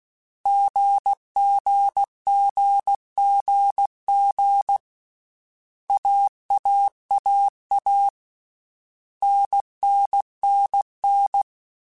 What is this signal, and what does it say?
Morse code, heard as a keyed tone, about 16 wpm, 800 Hz. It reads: GGGGG AAAA NNNN
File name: Basic Sets / Basic Letter Groups Collective7 / AWUNDG@13